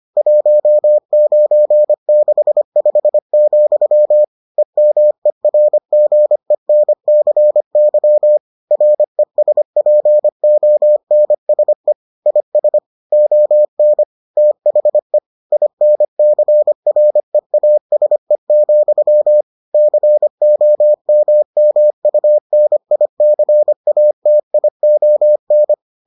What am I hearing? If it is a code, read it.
Morse code, heard as a keyed tone, about 25 wpm, 600 Hz. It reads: 1965, EMERGENCY RESPONSE IS ON THE INCREASE, COMMUNICATION